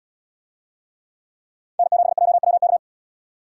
Morse code, heard as a keyed tone, about 65 wpm, 700 Hz. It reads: N7JWK